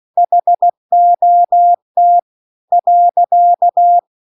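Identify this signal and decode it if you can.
Morse code, heard as a keyed tone, about 16 wpm, spaced as for 14 wpm, 700 Hz. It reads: HOT.
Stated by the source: Morse practice text at 16 WPM.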